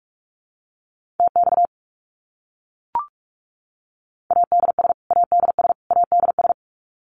Morse code, heard as a keyed tone, about 44 wpm, 700 Hz. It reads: TX ADS ADS ADS
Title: Abbreviations Repeated 3x - 45wpm